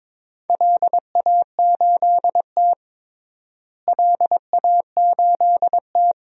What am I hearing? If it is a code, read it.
Morse code, heard as a keyed tone, about 22 wpm, 700 Hz. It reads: LA8T LA8T
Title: DXCC-100 Call Signs - 4 Characters - Repeated 3x - 22wpm